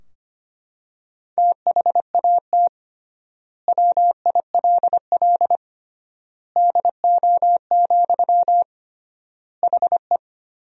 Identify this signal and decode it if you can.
Morse code, heard as a keyed tone, about 25 wpm, 700 Hz. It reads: THAT WILL DO, HE
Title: 3x Word-Spacing - Sentences from Top 100 Words - 25wpm